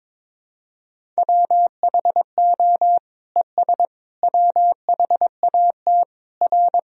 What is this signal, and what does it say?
Morse code, heard as a keyed tone, about 21 wpm, 700 Hz. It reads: WHO ES WHAT R
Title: Sentences from Top 100 Words and Abbreviations 22wpm